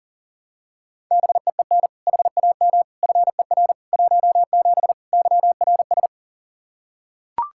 Morse code, heard as a keyed tone, about 40 wpm, 700 Hz. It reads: BEEN HAM FER 17 YRS
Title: Practice QSO - Exchange 3 - More Info - 40wpm